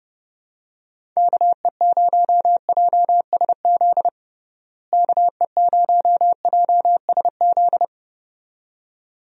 Morse code, heard as a keyed tone, about 30 wpm, 700 Hz. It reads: KE0JSZ KE0JSZ
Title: USA Call Signs - 6 Characters - Repeated 2x - 30wpm